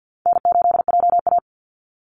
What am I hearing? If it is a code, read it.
Morse code, heard as a keyed tone, about 48 wpm, 700 Hz. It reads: N8JA
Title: Easy Call Signs Repeated 2x - 50wpm